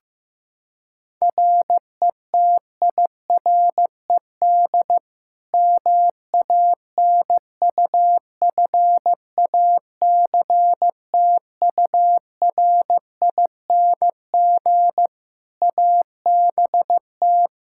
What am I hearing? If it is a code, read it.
Morse code, heard as a keyed tone, about 15 wpm, 700 Hz. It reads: RETIRED MANUFACTURING ABT
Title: K1USN SST QSO Practice 15wpm